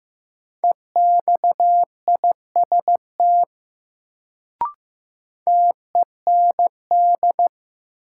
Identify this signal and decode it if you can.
Morse code, heard as a keyed tone, about 15 wpm, 700 Hz. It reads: EXIST TEND
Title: Sentences from Top 1000 Words - Review Part 5 - New Word Drill - 15wpm